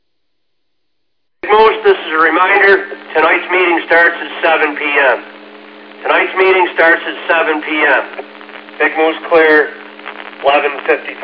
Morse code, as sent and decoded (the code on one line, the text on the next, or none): none